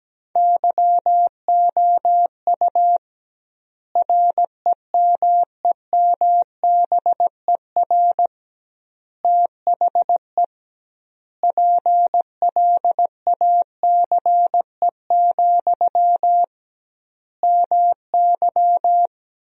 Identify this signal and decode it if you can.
Morse code, heard as a keyed tone, about 17 wpm, 700 Hz. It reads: YOU REMEMBER THE PLACE, MY
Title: 2x Word-Spacing - Sentences from Top 400 Words - 17wpm